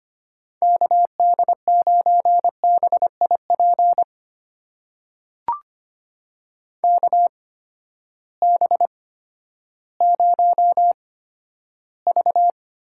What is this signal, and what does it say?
Morse code, heard as a keyed tone, about 25 wpm, spaced as for 13 wpm, 700 Hz. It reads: KD9BIP KB0V
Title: USA Call Signs - 6 Characters - 8x Character Spacing - 25wpm